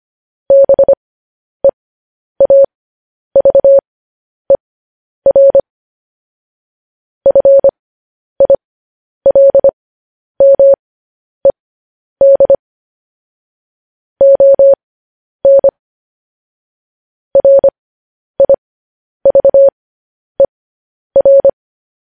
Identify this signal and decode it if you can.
Morse code, heard as a keyed tone, about 25 wpm, spaced as for 10 wpm, 550 Hz. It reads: BEAVER FILMED ON RIVER
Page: News Headlines in Morse Code at 10 WPM